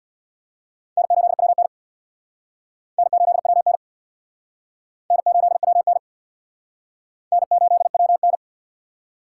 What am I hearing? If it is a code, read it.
Morse code, heard as a keyed tone, about 59 wpm, 700 Hz. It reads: N8WN N8WN N8WN N8WN
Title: Speed Racing - USA 1x2 Call Signs 50wpm